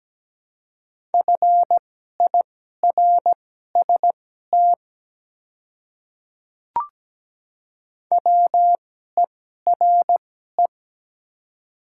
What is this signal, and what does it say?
Morse code, heard as a keyed tone, about 17 wpm, spaced as for 12 wpm, 700 Hz. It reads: FIRST WERE
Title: 2x Character Spacing - Top 100 Words - 17wpm